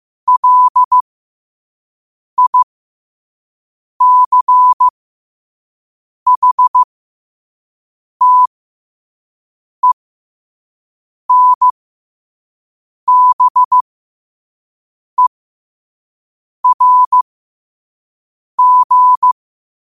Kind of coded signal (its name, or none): Morse code